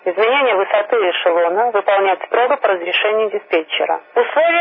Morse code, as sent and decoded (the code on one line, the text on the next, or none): none